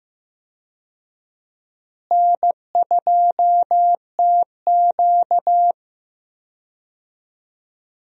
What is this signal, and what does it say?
Morse code, heard as a keyed tone, about 15 wpm, 700 Hz. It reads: N2TQ
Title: USA Call Signs - 4 Characters - Repeated 2x - 15wpm